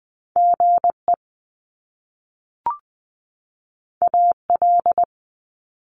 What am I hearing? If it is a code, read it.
Morse code, heard as a keyed tone, about 20 wpm, 700 Hz. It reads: GE AL
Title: Top Two Letter Combinations in Words 20wpm